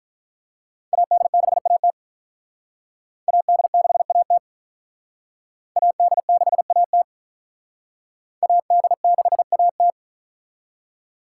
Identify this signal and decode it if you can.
Morse code, heard as a keyed tone, about 43 wpm, 700 Hz. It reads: AD6AT AD6AT AD6AT AD6AT
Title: Speed Racing - USA 2x2 Call Signs 35wpm